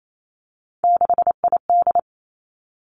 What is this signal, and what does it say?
Morse code, heard as a keyed tone, about 28 wpm, 700 Hz. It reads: 6ID
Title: Three Challenging Letter-Number Combinations 28wpm